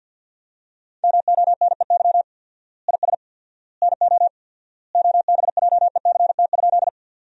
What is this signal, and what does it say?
Morse code, heard as a keyed tone, about 46 wpm, 700 Hz. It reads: MONEY IS NO OBJECT?
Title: 4x Word-Spacing - Sentences from Top 500 Words - 50wpm